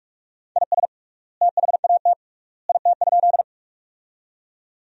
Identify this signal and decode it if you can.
Morse code, heard as a keyed tone, about 46 wpm, 690 Hz. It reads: IS THAT IT?
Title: 3x Word-Spacing - Sentences from Top 100 Words - 45wpm